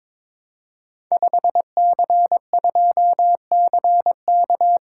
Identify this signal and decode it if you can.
Morse code, heard as a keyed tone, about 23 wpm, 690 Hz. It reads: 5C2CK
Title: Code Groups - 5 Letter:Numbers - 22wpm